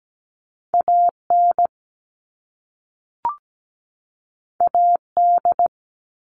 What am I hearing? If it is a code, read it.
Morse code, heard as a keyed tone, about 17 wpm, 700 Hz. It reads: AN AD
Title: All Cut Numbers - 2-Digit 17wpm